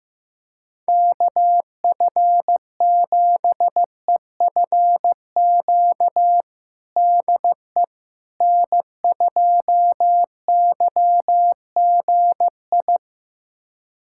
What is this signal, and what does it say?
Morse code, heard as a keyed tone, about 15 wpm, 700 Hz. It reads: KF7EFQ DE N2YGI
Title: USA Callsign DE Callsign 15wpm